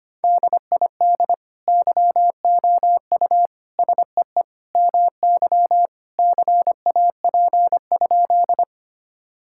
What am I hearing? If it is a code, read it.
Morse code, heard as a keyed tone, about 25 wpm, 700 Hz. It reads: DID YOU SEE MY CAP?
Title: Sentences from Top 1600 Words 25wpm